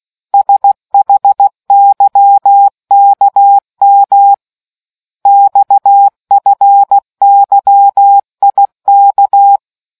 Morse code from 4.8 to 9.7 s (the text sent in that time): XFYIK